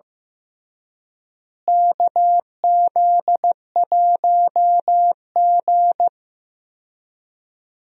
Morse code, heard as a keyed tone, about 15 wpm, 700 Hz. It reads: KZ1G
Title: USA Call Signs - 4 Characters - Repeated 3x - 15wpm